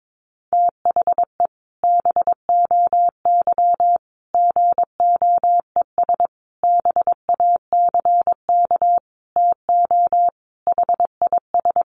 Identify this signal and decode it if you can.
Morse code, heard as a keyed tone, about 22 wpm, 700 Hz. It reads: THE BOY GOES BACK TO HIS
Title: Additional Easy Phrases 22wpm